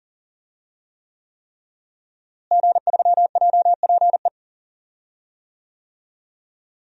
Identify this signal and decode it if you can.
Morse code, heard as a keyed tone, about 38 wpm, 700 Hz. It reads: G3JPE
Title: DXCC-100 Call Signs - 5 Characters - Repeated 3x - 40wpm